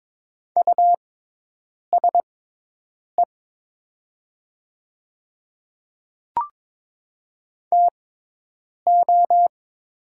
Morse code, heard as a keyed tone, about 22 wpm, spaced as for 8 wpm, 700 Hz. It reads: USE TO
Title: 6x Character Spacing - Top 100 Words - 22wpm